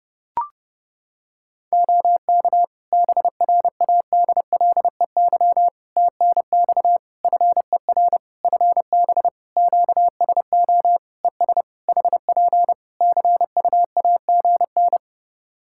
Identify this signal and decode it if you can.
Morse code, heard as a keyed tone, about 30 wpm, 700 Hz. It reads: OK BRADLEY TNX FER FB QSO ES HP CUAGN
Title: Practice QSO - Exchange 4 - Ending - 30wpm